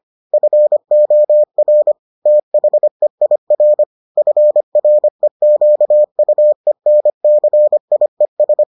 Morse code, heard as a keyed tone, about 25 wpm, 600 Hz. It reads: FOR THEIR FREQUENCIES